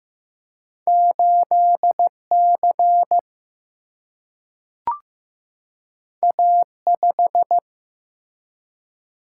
Morse code, heard as a keyed tone, about 15 wpm, 700 Hz. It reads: 8C A5
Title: Single-Letter Single-Number 15wpm